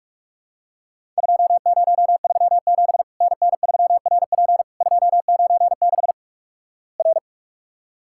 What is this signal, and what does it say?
Morse code, heard as a keyed tone, about 46 wpm, 700 Hz. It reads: 2037 NN3RP 296 R
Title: Worked All Europe DX Contest - QTC Exchange - 45wpm